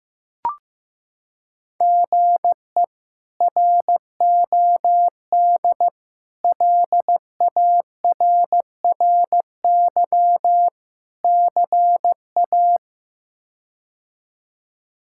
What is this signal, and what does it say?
Morse code, heard as a keyed tone, about 15 wpm, 700 Hz. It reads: GE ROD LARRY CA
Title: K1USN SST QSO Practice - 15wpm